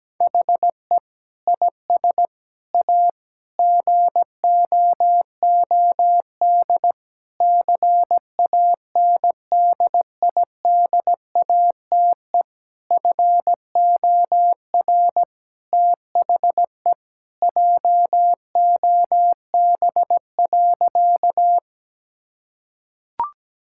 Morse code, heard as a keyed tone, about 17 wpm, 700 Hz. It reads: HE IS A GOOD CANDIDATE FOR THE JOB.
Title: Sentences from Top 1600 Words 17wpm